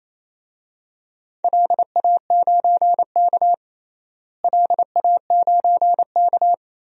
Morse code, heard as a keyed tone, about 28 wpm, 700 Hz. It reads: LA9K LA9K
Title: DXCC-100 Call Signs - 4 Characters - Repeated 2x - 28wpm